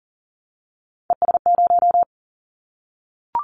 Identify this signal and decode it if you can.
Morse code, heard as a keyed tone, about 40 wpm, 700 Hz. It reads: ES0
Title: Callsign Prefixes for Top 50 DXCC - 40wpm